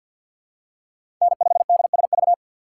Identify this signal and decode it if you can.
Morse code, heard as a keyed tone, about 51 wpm, 700 Hz. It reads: N5DSV
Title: USA Call Signs - 5 Characters - 2x Character Spacing - 50wpm